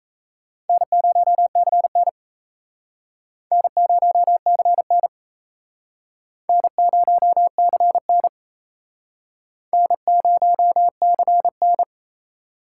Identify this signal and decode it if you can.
Morse code, heard as a keyed tone, about 36 wpm, 700 Hz. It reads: N0CN N0CN N0CN N0CN